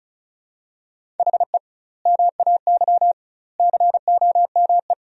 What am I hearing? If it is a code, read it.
Morse code, heard as a keyed tone, about 36 wpm, 700 Hz. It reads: HE MAY COME
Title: 2x Word-Spacing - Sentences from Top 300 Words - 35wpm